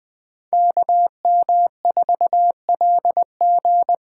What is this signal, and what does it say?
Morse code, heard as a keyed tone, about 20 wpm, 700 Hz. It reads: KM4LG
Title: USA Call Signs - 6 Characters - 8x Character Spacing - 20wpm